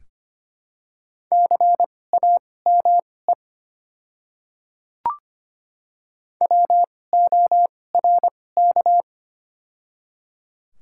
Morse code, heard as a keyed tone, about 25 wpm, spaced as for 18 wpm, 700 Hz. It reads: CAME WORK
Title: 2x Character Spacing - Top 300 Words - 25wpm